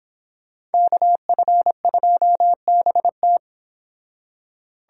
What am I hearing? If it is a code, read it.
Morse code, heard as a keyed tone, about 29 wpm, 700 Hz. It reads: KF2BT